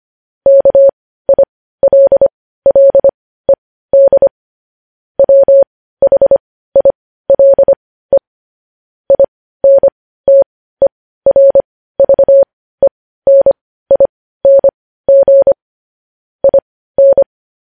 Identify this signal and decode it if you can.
Morse code, heard as a keyed tone, about 25 wpm, spaced as for 15 wpm, 550 Hz. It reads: KILLED WHILE INTERVENING IN
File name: podcast / News headlines-1x-25x15